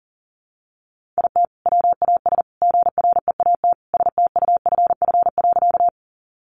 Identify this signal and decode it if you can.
Morse code, heard as a keyed tone, about 40 wpm, 700 Hz. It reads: IT WAS GREAT STUFF.
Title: Sentences from Top 1400 Words 40wpm